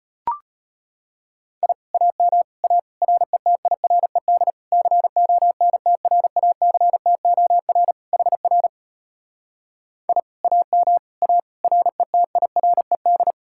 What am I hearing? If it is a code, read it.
Morse code, heard as a keyed tone, about 33 wpm, 700 Hz. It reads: I AM A RETIRED CONTRACTOR HR I AM A RETIRED